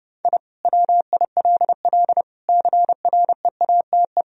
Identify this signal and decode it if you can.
Morse code, heard as a keyed tone, about 32 wpm, 700 Hz. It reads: I WILL CREATE
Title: Sentences from Top 1000 Words - Review Part 3 - 30wpm